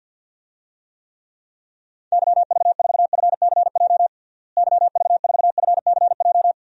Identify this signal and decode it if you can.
Morse code, heard as a keyed tone, about 49 wpm, 700 Hz. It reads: YV4FCJ YV4FCJ
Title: DXCC-100 Call Signs - 6 Characters - Repeated 3x - 50wpm